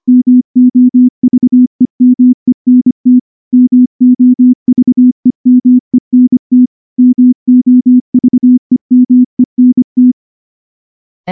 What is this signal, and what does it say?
Morse code, heard as a keyed tone, about 24 wpm, 250 Hz. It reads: MOVEMENT MOVEMENT MOVEMENT
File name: morse / morse-training-25WPM